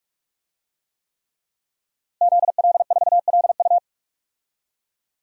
Morse code, heard as a keyed tone, about 46 wpm, 700 Hz. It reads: ZL4LU
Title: DXCC-100 Call Signs - 5 Characters - Repeated 3x - 45wpm